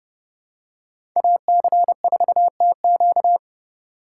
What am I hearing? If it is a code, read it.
Morse code, heard as a keyed tone, about 32 wpm, 700 Hz. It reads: AC4TQ
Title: USA Call Signs - 4x Character Spacing - 30wpm